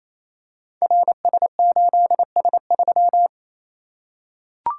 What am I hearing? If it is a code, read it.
Morse code, heard as a keyed tone, about 28 wpm, 700 Hz. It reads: RS8S3
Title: Code Groups - 5 Letter:Numbers - 28wpm